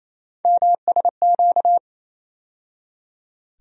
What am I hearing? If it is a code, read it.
Morse code, heard as a keyed tone, about 28 wpm, 700 Hz. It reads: MSQ
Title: USA Call Sign Postfixes - 3 Letters 28wpm